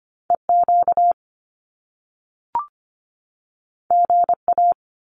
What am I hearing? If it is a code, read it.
Morse code, heard as a keyed tone, about 25 wpm, 700 Hz. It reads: EQ GA
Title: Top Two Letter Combinations in Words 25wpm